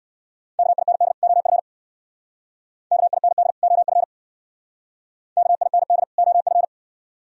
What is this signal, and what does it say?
Morse code, heard as a keyed tone, about 64 wpm, 700 Hz. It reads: KIND OF KIND OF KIND OF